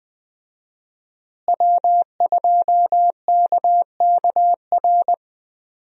Morse code, heard as a keyed tone, about 20 wpm, 700 Hz. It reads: W2KKR
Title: USA Call Signs - 5 Characters - Repeated 2x - 20wpm